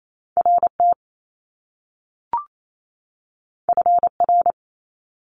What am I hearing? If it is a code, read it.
Morse code, heard as a keyed tone, about 28 wpm, 700 Hz. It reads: RT FR
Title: Top Two Letter Combinations in Words 28wpm